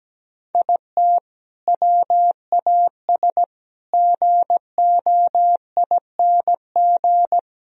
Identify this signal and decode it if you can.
Morse code, heard as a keyed tone, about 17 wpm, 700 Hz. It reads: IT WAS GOING
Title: Speed-Racing - Sets of 5 Words 17wpm